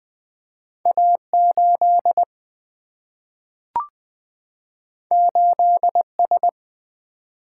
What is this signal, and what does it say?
Morse code, heard as a keyed tone, about 21 wpm, 700 Hz. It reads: A 8 8 S
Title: Single-Letter Single-Number 20wpm